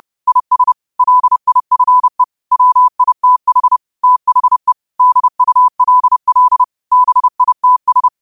Morse code, heard as a keyed tone, about 30 wpm, 1000 Hz. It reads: IS LIFE WITH THE DULL BITS